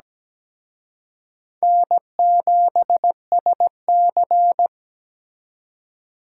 Morse code, heard as a keyed tone, about 20 wpm, 700 Hz. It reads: N7SC
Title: Speed Racing - USA 1x2 Call Signs 17wpm